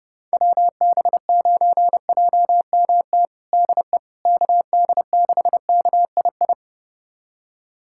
Morse code, heard as a keyed tone, about 32 wpm, 700 Hz. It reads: WB9JMT DE KD6KII
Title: USA Callsign DE Callsign - 6 Characters - 30wpm